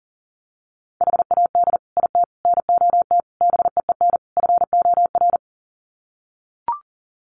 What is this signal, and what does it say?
Morse code, heard as a keyed tone, about 40 wpm, 700 Hz. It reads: HAD IT NOT BEEN FOR
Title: Sets of 5 Words Encore2 40wpm